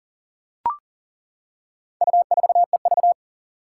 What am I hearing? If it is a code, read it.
Morse code, heard as a keyed tone, about 40 wpm, 700 Hz. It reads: U4EV
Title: Four Challenging Letter-Number Combinations 40wpm